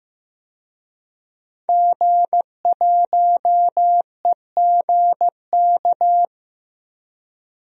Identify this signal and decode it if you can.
Morse code, heard as a keyed tone, about 15 wpm, 700 Hz. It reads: G1EGK
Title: DXCC-100 Call Signs - 5 Characters - 15wpm